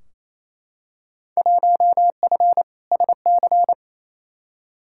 Morse code, heard as a keyed tone, about 28 wpm, 700 Hz. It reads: 1F SC
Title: Field Day Exchanges Encore - 28wpm